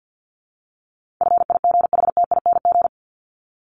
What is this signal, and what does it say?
Morse code, heard as a keyed tone, about 50 wpm, 700 Hz. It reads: FIGHTING
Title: ING Suffix - Top 100 Words with Base Words 50wpm